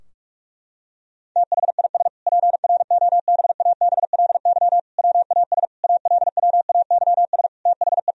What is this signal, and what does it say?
Morse code, heard as a keyed tone, about 47 wpm, 700 Hz. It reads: THIS PROBABLY WAS ALWAYS THE
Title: Sentences from Top 1000 Words - Review Part 4 - 45wpm